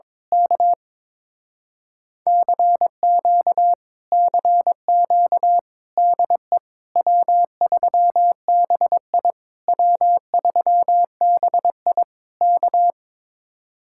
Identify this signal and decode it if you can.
Morse code, heard as a keyed tone, about 22 wpm, 700 Hz. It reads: K CQ CQ DE W3BI W3BI K
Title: Speed-Racing Common QSO Phrases - 22wpm